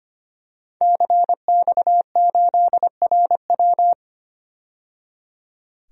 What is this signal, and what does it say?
Morse code, heard as a keyed tone, about 25 wpm, 700 Hz. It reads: CX8RW
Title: DXCC-100 Call Signs - 5 Characters - 25wpm